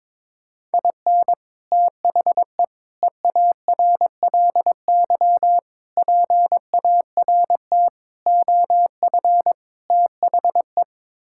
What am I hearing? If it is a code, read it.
Morse code, heard as a keyed tone, about 22 wpm, 700 Hz. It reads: IN THE EARLY PART OF THE
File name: practice / 22wpm / Sets of 6 Words - 22wpm